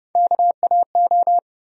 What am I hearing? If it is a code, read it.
Morse code, heard as a keyed tone, about 31 wpm, 700 Hz. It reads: KAO
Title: USA Call Sign Postfixes - 3 Letters 30wpm